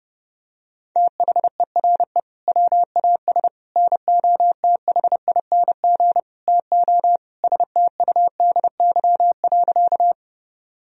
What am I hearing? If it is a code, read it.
Morse code, heard as a keyed tone, about 30 wpm, 700 Hz. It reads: THERE WAS NOTHING TO STUDY.
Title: Sentences from Top 1000 Words - Review Part 1 - 30wpm